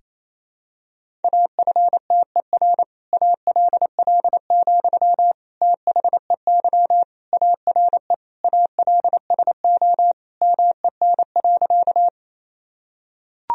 Morse code, heard as a keyed tone, about 28 wpm, 700 Hz. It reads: AFTER ALL, THEY ARE ALSO MEN.
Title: Sentences from Top 1000 Words - Review Part 1 - 28wpm